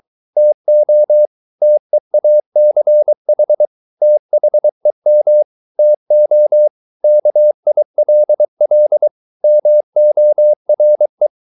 Morse code, heard as a keyed tone, about 23 wpm, 600 Hz. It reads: TO TEACH THEM TO KILL MORE